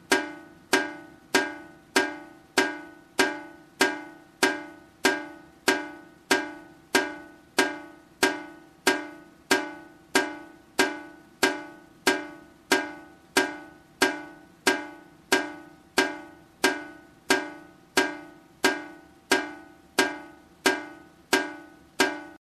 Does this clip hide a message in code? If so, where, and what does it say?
No.